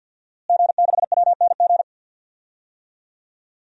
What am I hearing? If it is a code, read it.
Morse code, heard as a keyed tone, about 52 wpm, 700 Hz. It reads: G6WNG
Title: DXCC-100 Call Signs - 5 Characters - Repeated 3x - 50wpm